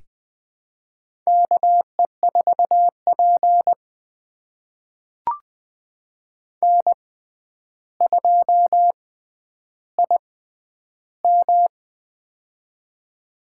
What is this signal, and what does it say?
Morse code, heard as a keyed tone, about 20 wpm, spaced as for 10 wpm, 700 Hz. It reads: KE4P N2IM
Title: USA Call Signs - 4 Characters - 6x Character Spacing - 20wpm